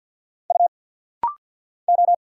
Morse code, heard as a keyed tone, about 50 wpm, 700 Hz. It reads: U O
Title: Single Letters - Rapid-Fire 50wpm